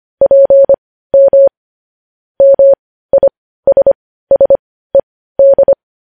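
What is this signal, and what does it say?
Morse code, heard as a keyed tone, about 25 wpm, spaced as for 15 wpm, 550 Hz. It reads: PM MISSED
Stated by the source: Morse code news headlines sent at 15 wpm.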